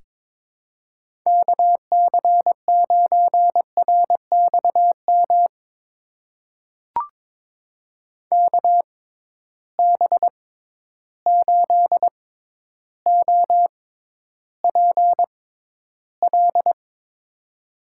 Morse code, heard as a keyed tone, about 22 wpm, spaced as for 11 wpm, 700 Hz. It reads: KC9RXM KB8OPL
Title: USA Call Signs - 6 Characters - 6x Character Spacing - 22wpm